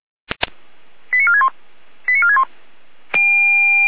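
Five-tone selective call (ZVEI), heard as quick runs of five tones, tones of about 70 ms. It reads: 89461 89461
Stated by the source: Five-tone ZVEI selective call received by SDR, from feuerwehr-orlamuende.org fire brigade audio archive.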